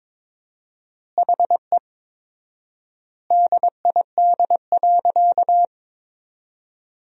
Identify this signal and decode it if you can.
Morse code, heard as a keyed tone, about 22 wpm, 700 Hz. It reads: HE DID.
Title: 4x Word-Spacing - Sentences from Top 500 Words - 22wpm